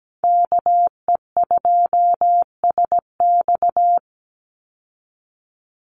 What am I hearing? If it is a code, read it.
Morse code, heard as a keyed tone, about 17 wpm, 700 Hz. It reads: KE2SX
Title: Call Signs Repeated 2x - 17wpm